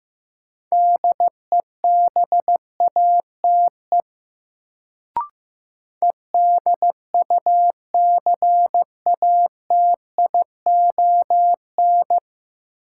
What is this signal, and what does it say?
Morse code, heard as a keyed tone, about 15 wpm, 700 Hz. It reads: DEBATE EDUCATION
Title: Sentences from Top 1100 Words - New Word Drill Alt - 15wpm